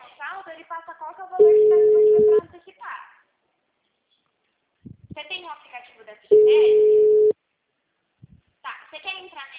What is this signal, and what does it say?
Ringback tone (Europe, 425 Hz), with speech.